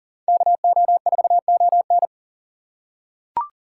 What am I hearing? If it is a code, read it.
Morse code, heard as a keyed tone, about 40 wpm, 700 Hz. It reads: KO4ON
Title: USA Call Signs - 5 Characters - 4x Character Spacing - 40wpm